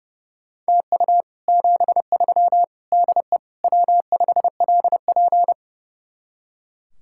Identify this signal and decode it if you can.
Morse code, heard as a keyed tone, about 39 wpm, 700 Hz. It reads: TU 73 DE W5LP